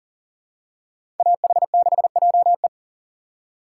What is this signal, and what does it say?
Morse code, heard as a keyed tone, about 40 wpm, 700 Hz. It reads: AH6JE
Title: USA Call Signs - 5 Characters - 2x Character Spacing - 40wpm